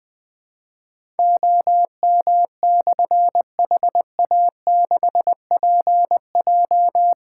Morse code, heard as a keyed tone, about 21 wpm, 700 Hz. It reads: OM/HA6PJ
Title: Difficult Call Signs 20wpm